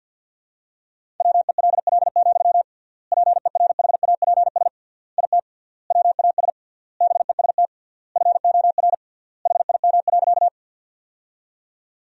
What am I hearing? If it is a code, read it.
Morse code, heard as a keyed tone, about 48 wpm, 700 Hz. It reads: WELL, PERHAPS IT WAS BEST FOR HIM.
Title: 3x Word-Spacing - Sentences from Top 500 Words - 50wpm